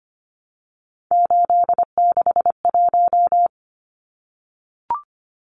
Morse code, heard as a keyed tone, about 25 wpm, 700 Hz. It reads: 861
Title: Numbers - 3-Digit 25wpm